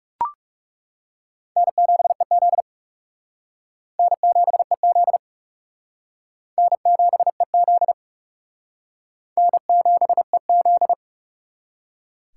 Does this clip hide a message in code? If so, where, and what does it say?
Yes: Morse code from 1.6 to 10.9 s, heard as a keyed tone, about 39 wpm, 700 Hz. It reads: N7EZ N7EZ N7EZ N7EZ